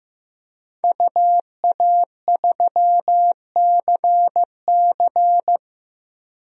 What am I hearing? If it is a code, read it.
Morse code, heard as a keyed tone, about 15 wpm, 700 Hz. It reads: UA3CC
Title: DXCC-100 Call Signs - 5 Characters - Repeated 2x - 15wpm